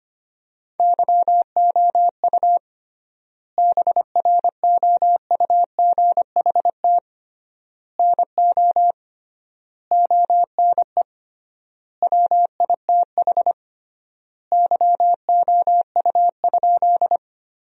Morse code, heard as a keyed tone, about 25 wpm, 700 Hz. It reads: YOU BROUGHT NO ONE WITH YOU?
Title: 3x Word-Spacing - Sentences from Top 500 Words - 25wpm